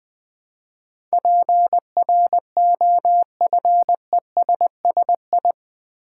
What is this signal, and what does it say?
Morse code, heard as a keyed tone, about 21 wpm, 700 Hz. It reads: PROFESSI
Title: Build A Word - 4 Letters At A Time - 10-Letter Words - 20wpm